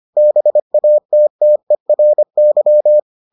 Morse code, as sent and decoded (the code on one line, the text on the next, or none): -... .- - - . .-. -.--
BATTERY